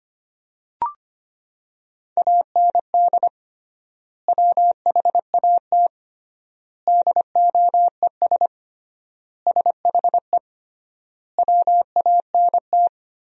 Morse code, heard as a keyed tone, about 25 wpm, 700 Hz. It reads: AND WHAT DOES SHE WANT
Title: 3x Word-Spacing - Sentences from Top 400 Words - 25wpm